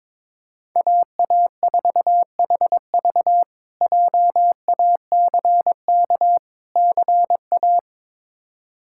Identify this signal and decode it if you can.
Morse code, heard as a keyed tone, about 22 wpm, 700 Hz. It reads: AA4HV JACK CA
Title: K1USN SST QSO Practice - 22wpm